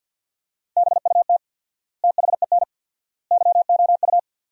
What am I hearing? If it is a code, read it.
Morse code, heard as a keyed tone, about 50 wpm, 700 Hz. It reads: BUT THEN YOU
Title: 4x Word-Spacing - Sentences from Top 300 Words - 50wpm